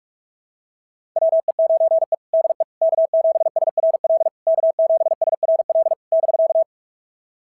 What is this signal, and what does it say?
Morse code, heard as a keyed tone, about 46 wpm, 690 Hz. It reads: WE9E DE K7SRL K7SRL <BK>
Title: SOTA QSO Practice 45wpm